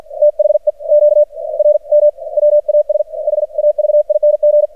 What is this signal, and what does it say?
Morse code, heard as a keyed tone, about 44 wpm, spaced as for 37 wpm, 600 Hz. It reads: THE QRM WAS HAVING
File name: archived / CRSnet